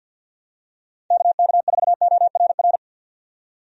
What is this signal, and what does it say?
Morse code, heard as a keyed tone, about 52 wpm, 700 Hz. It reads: KK4ORR
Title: USA Call Signs - 6 Characters - 50wpm